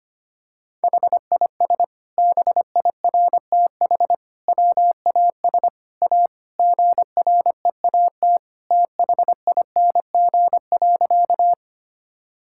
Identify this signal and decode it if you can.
Morse code, heard as a keyed tone, about 25 wpm, 700 Hz. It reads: HIS BIRTH WAS A GREAT THING.
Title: Sentences from Top 1000 Words - Review Part 5 - 25wpm